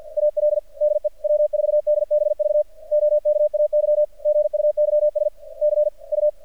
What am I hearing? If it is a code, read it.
Morse code, heard as a keyed tone, about 47 wpm, 590 Hz. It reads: AL DE W4DBV OKAY GUYS C U
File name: archived / CRSnet